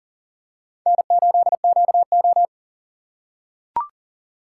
Morse code, heard as a keyed tone, about 40 wpm, 700 Hz. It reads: N8QO
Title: USA Call Signs - 4 Characters - 4x Character Spacing - 40wpm